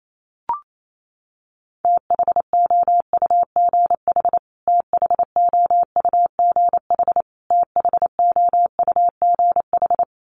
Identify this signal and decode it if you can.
Morse code, heard as a keyed tone, about 28 wpm, 700 Hz. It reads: THOUGH THOUGH THOUGH
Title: Top 500 Words Repeated 28wpm